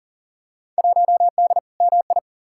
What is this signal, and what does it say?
Morse code, heard as a keyed tone, about 40 wpm, 700 Hz. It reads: 1D MI
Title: Field Day Exchanges - 40wpm